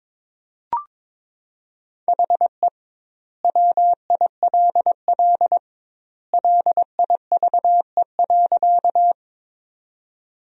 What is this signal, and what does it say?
Morse code, heard as a keyed tone, about 23 wpm, 700 Hz. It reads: HE WILL LIVE.
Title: 2x Word-Spacing - Sentences from Top 200 Words - 22wpm